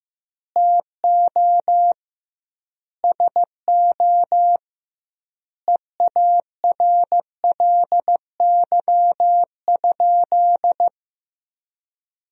Morse code, heard as a keyed tone, about 15 wpm, 700 Hz. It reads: TO SO EARLY?
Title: 2x Word-Spacing - Sentences from Top 400 Words - 15wpm